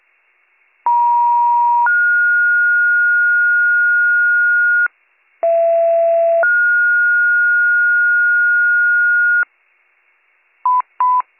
Two-tone page, heard as a one-second tone, then a three-second tone, about 950 then 1430 Hz, about 670 then 1430 Hz. It is heard from 0.9 to 9.4 s.